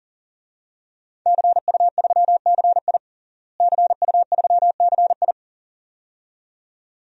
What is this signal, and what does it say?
Morse code, heard as a keyed tone, about 40 wpm, 700 Hz. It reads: CU3CI CU3CI
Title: DXCC-100 Call Signs - 5 Characters - Repeated 2x - 40wpm